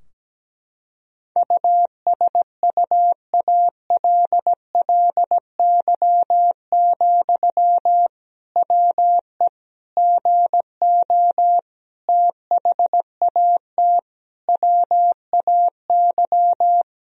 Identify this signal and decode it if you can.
Morse code, heard as a keyed tone, about 17 wpm, 700 Hz. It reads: USUALLY, WE GO THAT WAY
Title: Sentences from Top 1000 Words - Review Part 3 - 17wpm